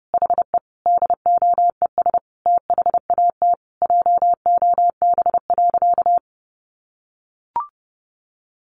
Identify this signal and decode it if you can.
Morse code, heard as a keyed tone, about 30 wpm, 700 Hz. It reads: HE DOES THAT JOB.
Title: Sentences from Top 700 Words 30wpm